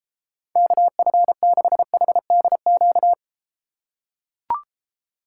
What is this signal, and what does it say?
Morse code, heard as a keyed tone, about 33 wpm, 700 Hz. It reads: KF6HDQ